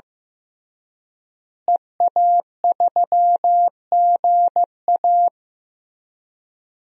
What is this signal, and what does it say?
Morse code, heard as a keyed tone, about 16 wpm, 700 Hz. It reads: EA3GA
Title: DXCC-100 Call Signs - 5 Characters - Repeated 3x - 15wpm